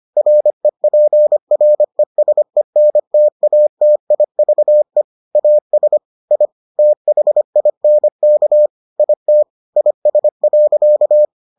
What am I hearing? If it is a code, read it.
Morse code, heard as a keyed tone, about 25 wpm, 600 Hz. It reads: REPRESENTATIVE AS I THINK IT IS.